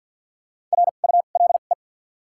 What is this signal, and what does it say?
Morse code, heard as a keyed tone, about 47 wpm, spaced as for 35 wpm, 700 Hz. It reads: RULE
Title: 2x Character Spacing - Top 500 Words - 50wpm